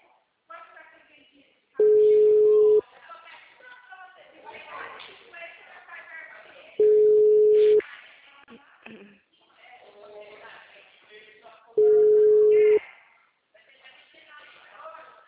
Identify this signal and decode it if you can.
Ringback tone (Europe, 425 Hz), with speech.